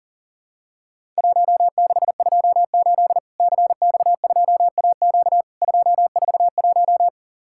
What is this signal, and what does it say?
Morse code, heard as a keyed tone, about 43 wpm, 700 Hz. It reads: 1628 CX2AQ 241
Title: Worked All Europe DX Contest - QTC Exchange - 40wpm